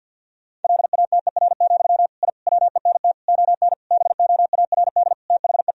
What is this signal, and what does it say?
Morse code, heard as a keyed tone, about 50 wpm, 700 Hz. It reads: LATER, I WENT ON BOARD THE
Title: Sentences from Top 1000 Words - Review Part 4 - 50wpm